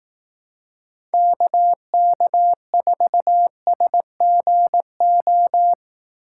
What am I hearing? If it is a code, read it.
Morse code, heard as a keyed tone, about 18 wpm, 700 Hz. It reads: KK4SGO
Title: Speed Racing - USA 2x3 Call Signs 15wpm